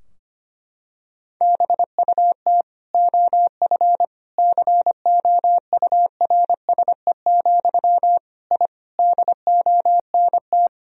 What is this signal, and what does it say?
Morse code, heard as a keyed tone, about 25 wpm, 700 Hz. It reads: BUT OF COURSE, I DONT
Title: Sentences from Top 1000 Words - Review Part 2 - 25wpm